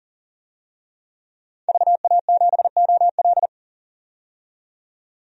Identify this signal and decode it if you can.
Morse code, heard as a keyed tone, about 40 wpm, 700 Hz. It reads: VA7OL
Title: DXCC-100 Call Signs - 5 Characters - 40wpm